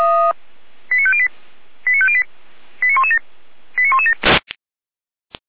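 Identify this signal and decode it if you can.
Five-tone selective call (ZVEI), heard as quick runs of five tones, tones of about 70 ms. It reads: 89508 89508 89117 89117
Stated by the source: ZVEI five-tone selective call received by SDR.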